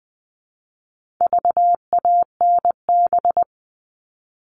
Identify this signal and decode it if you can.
Morse code, heard as a keyed tone, about 20 wpm, 700 Hz. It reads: VANB
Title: All Cut Numbers - 4-Digit 20wpm